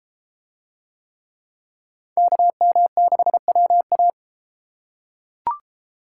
Morse code, heard as a keyed tone, about 31 wpm, 700 Hz. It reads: KM6WA